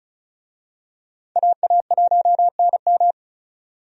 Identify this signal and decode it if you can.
Morse code, heard as a keyed tone, about 36 wpm, 700 Hz. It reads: AA1NM
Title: USA Call Signs - 5 Characters - Repeated 3x - 35wpm